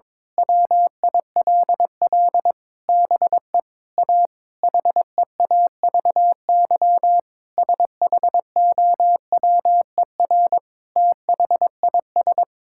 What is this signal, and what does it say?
Morse code, heard as a keyed tone, about 21 wpm, 700 Hz. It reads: WILL BE A HEAVY SHOWER THIS
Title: Sentences from Top 1600 Words 22wpm